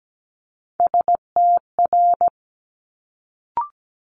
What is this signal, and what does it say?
Morse code, heard as a keyed tone, about 17 wpm, 700 Hz. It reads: STR
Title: Multi-letter Phonemes Quick Fire 17wpm